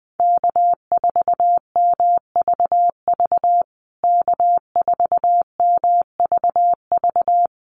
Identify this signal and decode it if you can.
Morse code, heard as a keyed tone, about 20 wpm, 700 Hz. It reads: K4MVV K4MVV
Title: USA Call Signs Repeated 3x 20wpm